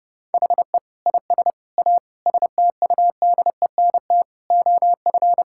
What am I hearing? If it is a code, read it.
Morse code, heard as a keyed tone, about 29 wpm, 700 Hz. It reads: HE IS A STUDENT OF
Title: Sentences from Top 1000 Words - Review Part 6 - 30wpm